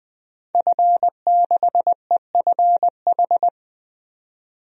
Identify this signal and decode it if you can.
Morse code, heard as a keyed tone, about 20 wpm, 700 Hz. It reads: F6EFH
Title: DXCC-100 Call Signs - 5 Characters - 20wpm